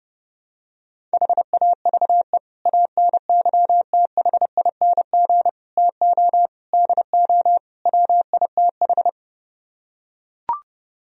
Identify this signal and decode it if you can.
Morse code, heard as a keyed tone, about 31 wpm, 700 Hz. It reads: HAVE ANYTHING TO DO WITH